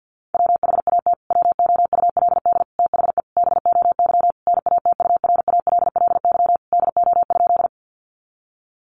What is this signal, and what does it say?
Morse code, heard as a keyed tone, about 50 wpm, 700 Hz. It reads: WHAT WOULD THE BOY NATURALLY DO?
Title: Sentences from Top 1300 Words 50wpm